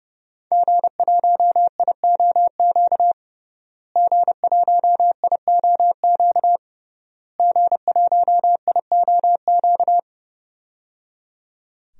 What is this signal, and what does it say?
Morse code, heard as a keyed tone, about 30 wpm, 700 Hz. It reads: G1IOQ G1IOQ G1IOQ
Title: DXCC-100 Call Signs - 5 Characters - Repeated 3x - 30wpm